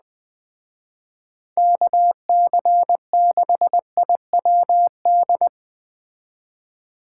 Morse code, heard as a keyed tone, about 19 wpm, 700 Hz. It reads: KC6IWD
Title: Speed Racing - USA 2x3 Call Signs 17wpm